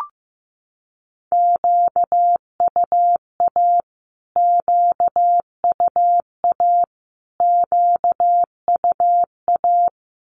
Morse code, heard as a keyed tone, about 15 wpm, 700 Hz. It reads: QUA QUA QUA
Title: Q-Codes Repeated 3x 15wpm